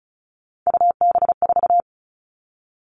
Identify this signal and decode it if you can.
Morse code, heard as a keyed tone, about 36 wpm, 700 Hz. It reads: UB4
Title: Three Challenging Letter-Number Combinations 35wpm